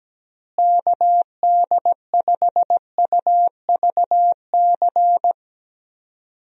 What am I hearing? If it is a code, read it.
Morse code, heard as a keyed tone, about 17 wpm, 700 Hz. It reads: KD5UVC
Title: USA Call Signs - 6 Characters - Repeated 3x - 17wpm